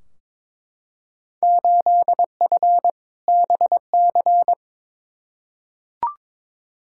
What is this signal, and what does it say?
Morse code, heard as a keyed tone, about 22 wpm, 700 Hz. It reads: 8F BC
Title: Field Day Exchanges Encore - 22wpm